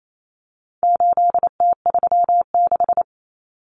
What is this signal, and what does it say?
Morse code, heard as a keyed tone, about 28 wpm, 700 Hz. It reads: 8T36
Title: Common Cut Numbers - 4-Digit 28wpm